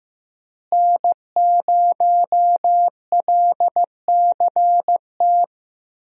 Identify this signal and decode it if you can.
Morse code, heard as a keyed tone, about 15 wpm, 700 Hz. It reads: N0LCT
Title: USA Call Signs - 5 Characters - Repeated 2x - 15wpm